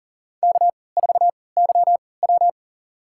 Morse code, heard as a keyed tone, about 40 wpm, spaced as for 23 wpm, 700 Hz. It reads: K4YW
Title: USA Call Signs - 4 Characters - 3x Character Spacing - 40wpm